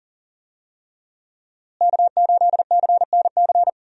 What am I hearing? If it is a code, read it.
Morse code, heard as a keyed tone, about 39 wpm, 700 Hz. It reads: K8CNC
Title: USA Call Signs - 5 Characters - Repeated 3x - 40wpm